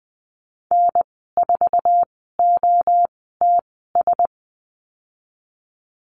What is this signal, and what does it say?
Morse code, heard as a keyed tone, about 20 wpm, spaced as for 15 wpm, 700 Hz. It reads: N4OTS
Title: 062 Lesson 11g - Letter h - Call Signs 20wpm